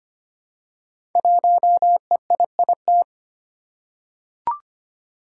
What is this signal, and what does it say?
Morse code, heard as a keyed tone, about 25 wpm, 700 Hz. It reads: 1EIIT
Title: Five Challenging Letter-Number Combinations 25wpm